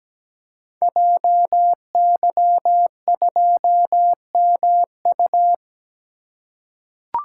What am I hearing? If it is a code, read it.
Morse code, heard as a keyed tone, about 17 wpm, 700 Hz. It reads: JY2MU
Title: Code Groups - 5 Letter:Numbers - 17wpm